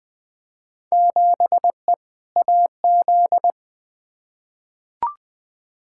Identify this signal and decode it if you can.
Morse code, heard as a keyed tone, about 20 wpm, 700 Hz. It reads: 7E AZ
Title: Field Day Exchanges Encore - 20wpm